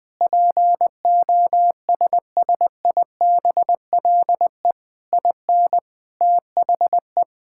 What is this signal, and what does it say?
Morse code, heard as a keyed tone, about 19 wpm, 700 Hz. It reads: POSSIBLE IN THE
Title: Sentences from Top 1000 Words - Review Part 6 - 20wpm